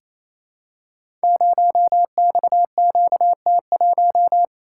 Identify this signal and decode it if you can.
Morse code, heard as a keyed tone, about 28 wpm, 700 Hz. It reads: 0XQT1
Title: Code Groups - 5 Letter:Numbers - 28wpm